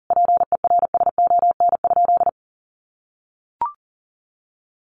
Morse code, heard as a keyed tone, about 40 wpm, 700 Hz. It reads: PERSON?
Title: Sentences from Top 1300 Words 40wpm